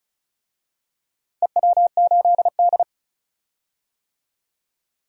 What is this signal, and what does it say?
Morse code, heard as a keyed tone, about 36 wpm, 700 Hz. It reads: EW8D
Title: DXCC-100 Call Signs - 4 Characters - 35wpm